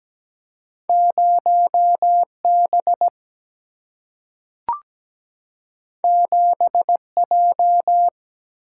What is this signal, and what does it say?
Morse code, heard as a keyed tone, about 17 wpm, 700 Hz. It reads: 0B 7J
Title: Single-Letter Single-Number 17wpm